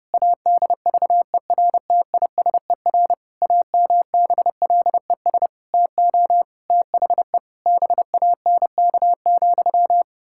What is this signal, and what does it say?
Morse code, heard as a keyed tone, about 30 wpm, 700 Hz. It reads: ADVERTISER AMBLES TO THE BANK,
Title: Pangram Challenge v2 - 30wpm